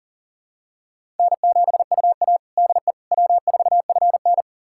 Morse code, heard as a keyed tone, about 40 wpm, 700 Hz. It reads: N7UA DE W4FN
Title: USA Callsign DE Callsign - 4 Characters - 40wpm